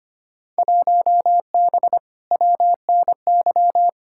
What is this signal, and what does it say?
Morse code, heard as a keyed tone, about 25 wpm, 700 Hz. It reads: 1B WNY
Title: Field Day Exchanges - 25wpm